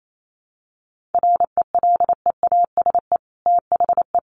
Morse code, heard as a keyed tone, about 28 wpm, 700 Hz. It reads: RELEASE THE
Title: Sentences from Top 1100 Words 28wpm